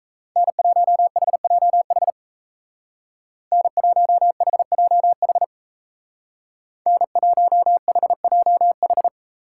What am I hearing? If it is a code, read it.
Morse code, heard as a keyed tone, about 33 wpm, 700 Hz. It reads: N1HJH N1HJH N1HJH
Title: Speed Racing - USA 1x3 Call Signs 28wpm